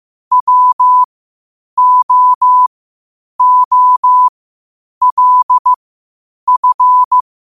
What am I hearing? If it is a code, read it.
Morse code, heard as a keyed tone, about 15 wpm, spaced as for 8 wpm, 1000 Hz. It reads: WOOLF
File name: qotdfiles / 8wpm